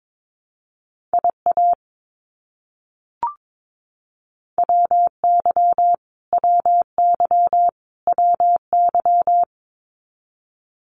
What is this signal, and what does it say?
Morse code, heard as a keyed tone, about 22 wpm, 700 Hz. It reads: IA WY WY WY
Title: US States Repeated 3x 22wpm